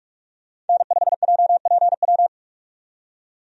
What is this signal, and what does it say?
Morse code, heard as a keyed tone, about 42 wpm, 700 Hz. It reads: N5JPW
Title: USA Call Signs - 5 Characters - 45wpm